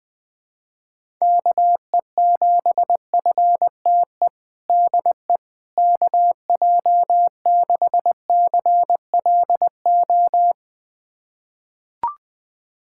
Morse code, heard as a keyed tone, about 20 wpm, 700 Hz. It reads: KE7FTE DE KJ6CLO
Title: USA Callsign DE Callsign - 6 Characters - 20wpm